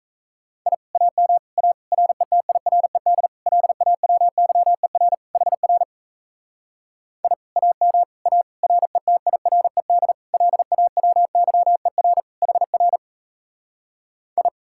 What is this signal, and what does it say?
Morse code, heard as a keyed tone, about 38 wpm, 700 Hz. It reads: I AM A RETIRED LAWYER HR I AM A RETIRED LAWYER HR I